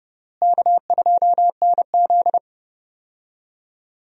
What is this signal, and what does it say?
Morse code, heard as a keyed tone, about 30 wpm, 700 Hz. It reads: K2NZ
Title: USA Call Signs - 4 Characters - Repeated 3x - 30wpm